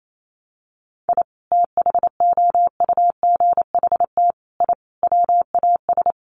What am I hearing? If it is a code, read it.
Morse code, heard as a keyed tone, about 28 wpm, 700 Hz. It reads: I THOUGHT I WAS
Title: Sets of 5 Words Encore2 28wpm